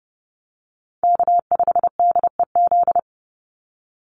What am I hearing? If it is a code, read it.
Morse code, heard as a keyed tone, about 30 wpm, 700 Hz. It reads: K5DEZ
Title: NA SOTA Chaser Callsigns 30wpm